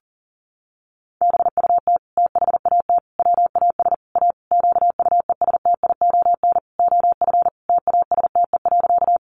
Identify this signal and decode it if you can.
Morse code, heard as a keyed tone, about 40 wpm, 700 Hz. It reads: BUT THAT WAS A QUESTION OF TASTE.
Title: Sentences from Top 1100 Words 40wpm